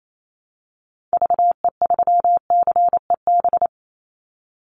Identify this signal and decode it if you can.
Morse code, heard as a keyed tone, about 28 wpm, 700 Hz. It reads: VE3CEB
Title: Call Signs - 28wpm